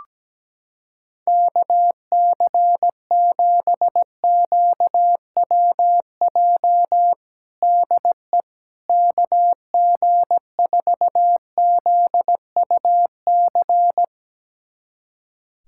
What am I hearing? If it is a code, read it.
Morse code, heard as a keyed tone, about 17 wpm, 700 Hz. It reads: KC7QWJ DE KG4ZUC
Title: USA Callsign DE Callsign - 6 Characters - 17wpm